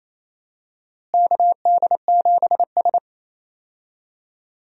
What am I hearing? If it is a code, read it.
Morse code, heard as a keyed tone, about 28 wpm, 700 Hz. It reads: KD7S
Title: USA Call Signs - 4 Characters - Repeated 3x - 28wpm